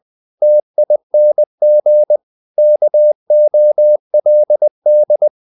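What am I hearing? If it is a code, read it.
Morse code, heard as a keyed tone, about 20 wpm, 600 Hz. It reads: TING KOLD
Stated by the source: Morse code practice of short Danish words at 20 wpm, from CW med Gnister.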